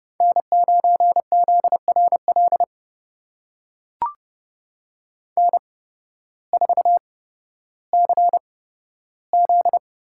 Morse code, heard as a keyed tone, about 30 wpm, spaced as for 14 wpm, 700 Hz. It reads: N9ZRL N4CZ
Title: USA Call Signs - 8x Character Spacing - 30wpm